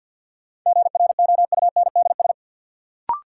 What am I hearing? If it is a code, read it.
Morse code, heard as a keyed tone, about 53 wpm, 700 Hz. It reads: GROUNDS
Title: Sentences from Top 1300 Words - New Word Drill - 50wpm